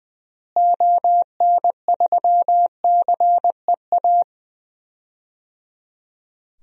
Morse code, heard as a keyed tone, about 20 wpm, 700 Hz. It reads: ON3CEA
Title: DXCC-100 Call Signs - 6 Characters - Repeated 3x - 20wpm